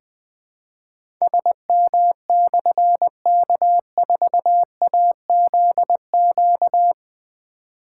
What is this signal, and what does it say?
Morse code, heard as a keyed tone, about 20 wpm, 700 Hz. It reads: SM/K4AZQ
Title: Difficult Call Signs 20wpm